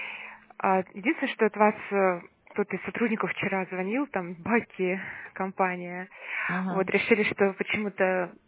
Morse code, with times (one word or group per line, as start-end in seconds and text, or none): none